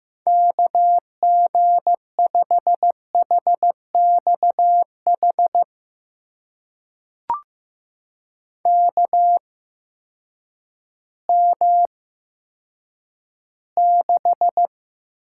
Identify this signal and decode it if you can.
Morse code, heard as a keyed tone, about 15 wpm, spaced as for 9 wpm, 700 Hz. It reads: KG5HXH KM6
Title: USA Call Signs - 6 Characters - 8x Character Spacing - 15wpm